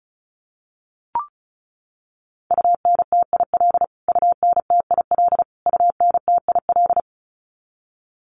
Morse code, heard as a keyed tone, about 34 wpm, 700 Hz. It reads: UNTIL UNTIL UNTIL
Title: Top 300 Words Repeated 35wpm